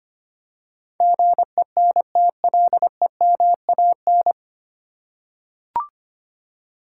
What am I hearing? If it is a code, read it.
Morse code, heard as a keyed tone, about 25 wpm, 700 Hz. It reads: GENTLEMAN
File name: practice / 25wpm / 9 Letter Words - First 3 Letters Given - 25wpm